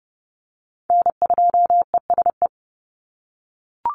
Morse code, heard as a keyed tone, about 30 wpm, 700 Hz. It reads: N2ESE
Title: NA SOTA Chaser Callsigns 30wpm